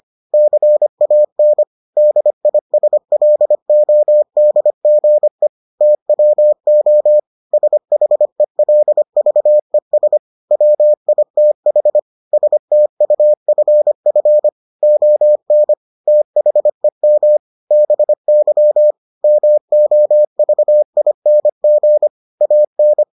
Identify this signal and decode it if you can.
Morse code, heard as a keyed tone, about 25 wpm, 600 Hz. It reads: CAN DISLODGE TWO SHELVES WITH STUFF ON THEM BY MOVING AN